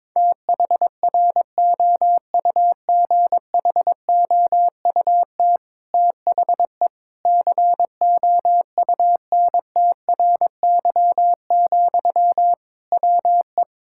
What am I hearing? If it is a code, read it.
Morse code, heard as a keyed tone, about 22 wpm, 700 Hz. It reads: THROUGHOUT THE COUNTRY, WE
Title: Sentences from Top 1000 Words - Review Part 5 - 22wpm